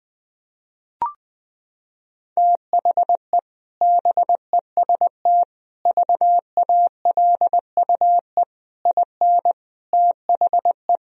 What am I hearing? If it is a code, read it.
Morse code, heard as a keyed tone, about 20 wpm, 700 Hz. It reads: THE BEST VALUE IN THE
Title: Sentences from Top 1000 Words - Review Part 3 - 20wpm